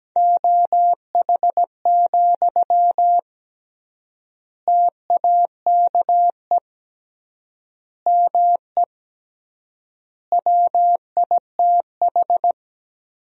Morse code, heard as a keyed tone, about 17 wpm, 700 Hz. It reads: OH, TAKE ME WITH
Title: 3x Word-Spacing - Sentences from Top 500 Words - 17wpm